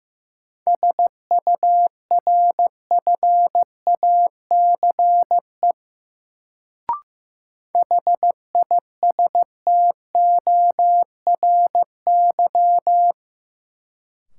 Morse code, heard as a keyed tone, about 15 wpm, 700 Hz. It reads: SURFACE HISTORY
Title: Sentences from Top 600 Words - New Word Drill - 15wpm